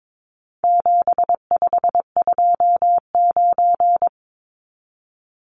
Morse code, heard as a keyed tone, about 22 wpm, 700 Hz. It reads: 7529
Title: Numbers - 4-Digit 22wpm